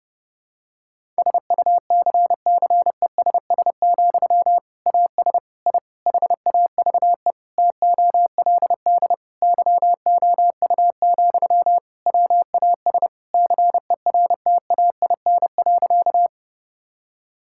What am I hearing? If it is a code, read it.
Morse code, heard as a keyed tone, about 32 wpm, 700 Hz. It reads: SUCCESS, AS I HAVE TOLD YOU, WAS CERTAIN.
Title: Sentences from Top 1000 Words - Review Part 6 - 30wpm